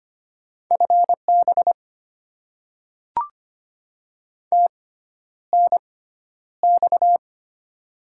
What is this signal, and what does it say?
Morse code, heard as a keyed tone, about 25 wpm, spaced as for 11 wpm, 700 Hz. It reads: FB TNX
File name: Top 20 QSO Elements - 6x Character Spacing - 25wpm